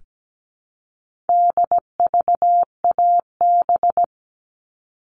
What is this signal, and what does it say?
Morse code, heard as a keyed tone, about 17 wpm, 700 Hz. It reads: DVAB
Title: All Cut Numbers - 4-Digit 17wpm